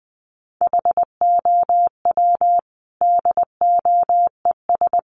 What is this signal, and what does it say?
Morse code, heard as a keyed tone, about 20 wpm, 700 Hz. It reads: HOW DOES
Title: Sentences from Top 400 Words 20wpm